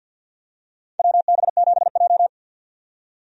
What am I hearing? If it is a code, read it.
Morse code, heard as a keyed tone, about 51 wpm, 700 Hz. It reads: WB7J
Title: USA Call Signs - 4 Characters - 50wpm